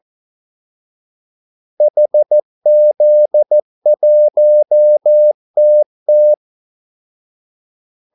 Morse code, heard as a keyed tone, about 14 wpm, 600 Hz. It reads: HZ1TT
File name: Callsigns-14wpm